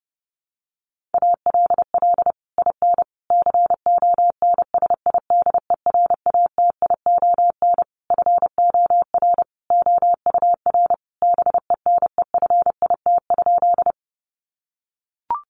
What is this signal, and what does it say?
Morse code, heard as a keyed tone, about 28 wpm, 700 Hz. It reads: ALL IN CONSIDERATION FOR OUR BENEFIT?
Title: Sentences from Top 1500 Words 30wpm